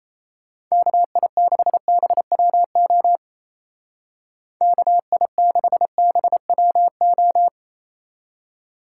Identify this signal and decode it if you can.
Morse code, heard as a keyed tone, about 35 wpm, 700 Hz. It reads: KI6BWO KI6BWO
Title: Speed Racing - USA 2x3 Call Signs 28wpm